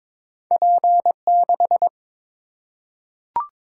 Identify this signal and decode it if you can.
Morse code, heard as a keyed tone, about 22 wpm, 700 Hz. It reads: P6
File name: practice / 22wpm / Two Letter-Number Combinations - 22wpm